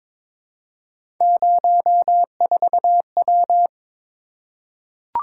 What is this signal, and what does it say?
Morse code, heard as a keyed tone, about 22 wpm, 700 Hz. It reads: 04W
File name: practice / 22wpm / Three Letter-Number Combinations - 22wpm